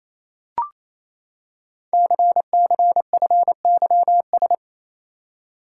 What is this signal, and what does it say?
Morse code, heard as a keyed tone, about 29 wpm, 700 Hz. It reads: CCFYS
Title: Code Groups - 5 Letters - 28wpm